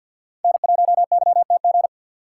Morse code, heard as a keyed tone, about 46 wpm, 700 Hz. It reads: N1YTG
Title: USA Call Signs - 5 Characters - 3x Character Spacing - 50wpm